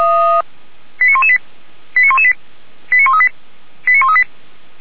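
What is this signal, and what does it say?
Five-tone selective call (ZVEI), heard as quick runs of five tones, tones of about 70 ms. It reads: 89118 89118 89138 89138
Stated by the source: ZVEI five-tone call 89227 received by SDR, feuerwehr-orlamuende.org fire brigade audio archive.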